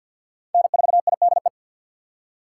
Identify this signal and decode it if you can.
Morse code, heard as a keyed tone, about 53 wpm, 700 Hz. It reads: N4IDE
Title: USA Call Signs - 5 Characters - 3x Character Spacing - 50wpm